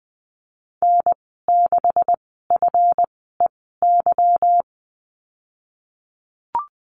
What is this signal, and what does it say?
Morse code, heard as a keyed tone, about 20 wpm, spaced as for 15 wpm, 700 Hz. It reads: N6FEY
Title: 213 Lesson 34g - Number 8 - Call Signs 20wpm